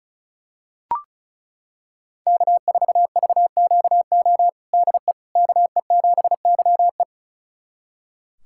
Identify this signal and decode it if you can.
Morse code, heard as a keyed tone, about 36 wpm, 700 Hz. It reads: K4VQO DE KE7YE
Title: USA Callsign DE Callsign - 5 Characters - 35wpm